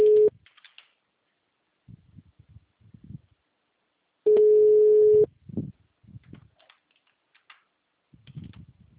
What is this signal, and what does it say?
Ringback tone (Europe, 425 Hz).
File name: out- -2011- - - .14